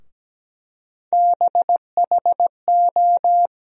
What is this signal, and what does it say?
Morse code, heard as a keyed tone, about 17 wpm, 700 Hz. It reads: BHO
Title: USA Call Sign Postfixes - 3 Letters 17wpm